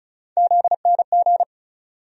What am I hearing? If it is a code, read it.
Morse code, heard as a keyed tone, about 36 wpm, 700 Hz. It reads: ZNG
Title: USA Call Sign Postfixes - 3 Letters 35wpm